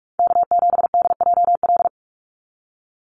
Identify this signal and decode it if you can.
Morse code, heard as a keyed tone, about 44 wpm, 700 Hz. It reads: K7DJL
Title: NA SOTA Chaser Callsigns 45wpm